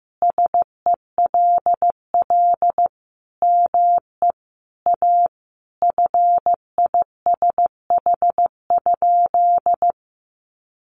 Morse code, heard as a keyed tone, about 15 wpm, 700 Hz. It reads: SELL ME A FISH?
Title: Sentences from Top 1100 Words 15wpm